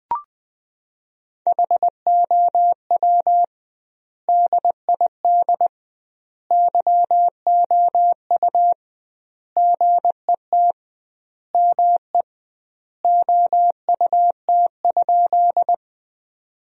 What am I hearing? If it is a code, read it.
Morse code, heard as a keyed tone, about 20 wpm, 700 Hz. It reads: HOW DID YOU GET ME OUT?
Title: 2x Word-Spacing - Sentences from Top 200 Words - 20wpm